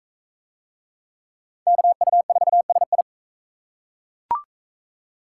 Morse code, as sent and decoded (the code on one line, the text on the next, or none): -.- ..- ....- ... ..
KU4SI